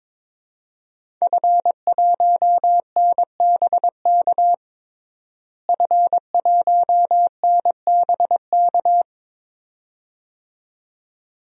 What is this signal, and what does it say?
Morse code, heard as a keyed tone, about 22 wpm, 700 Hz. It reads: F1NBK F1NBK
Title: DXCC-100 Call Signs - 5 Characters - Repeated 2x - 22wpm